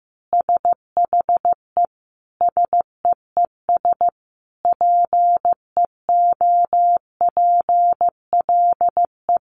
Morse code, heard as a keyed tone, about 15 wpm, 700 Hz. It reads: SHE SEES PEOPLE
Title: Easy Phrases 15wpm